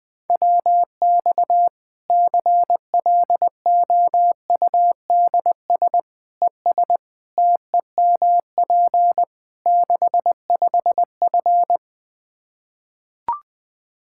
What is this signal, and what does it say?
Morse code, heard as a keyed tone, about 20 wpm, 700 Hz. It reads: WX CLOUDS ES TEMP 65F
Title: Practice QSO - Exchange 2 - Info - 20wpm